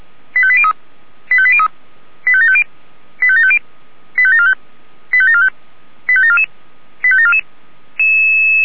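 Five-tone selective call (ZVEI), heard as quick runs of five tones, tones of about 70 ms. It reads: 85803 85803 85750 85750 85745 85745 85744 85744
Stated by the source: Five-tone ZVEI selective call received by SDR.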